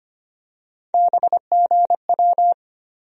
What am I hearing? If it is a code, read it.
Morse code, heard as a keyed tone, about 25 wpm, 700 Hz. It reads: BGW
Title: USA Call Sign Postfixes - 3 Letters 25wpm